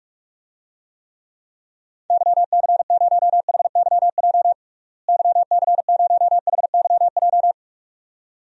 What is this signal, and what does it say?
Morse code, heard as a keyed tone, about 42 wpm, 700 Hz. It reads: YC0HYJ YC0HYJ
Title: DXCC-100 Call Signs - 6 Characters - Repeated 2x - 45wpm